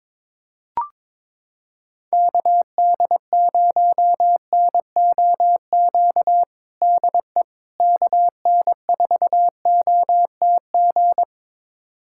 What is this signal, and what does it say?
Morse code, heard as a keyed tone, about 22 wpm, 700 Hz. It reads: KD0NOQ DE KN4OTG
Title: USA Callsign DE Callsign - 6 Characters - 22wpm